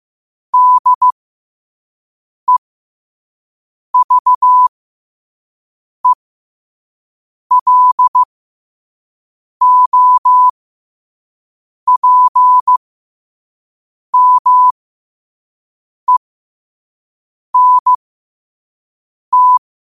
Morse code, heard as a keyed tone, about 15 wpm, spaced as for 5 wpm, 1000 Hz. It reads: DEVELOPMENT
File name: qotdfiles / 5wpm